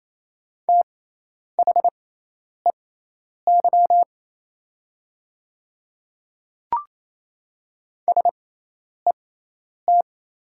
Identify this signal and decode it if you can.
Morse code, heard as a keyed tone, about 27 wpm, spaced as for 10 wpm, 700 Hz. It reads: THEY SET
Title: 6x Character Spacing - Top 300 Words - 28wpm